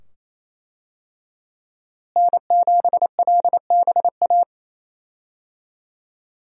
Morse code, heard as a keyed tone, about 28 wpm, 700 Hz. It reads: N7LBA
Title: USA Call Signs - 5 Characters - Repeated 2x - 28wpm